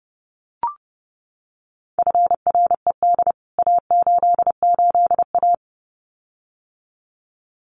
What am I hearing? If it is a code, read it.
Morse code, heard as a keyed tone, about 30 wpm, 700 Hz. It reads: FRED A88A
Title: CWT Exchange 30wpm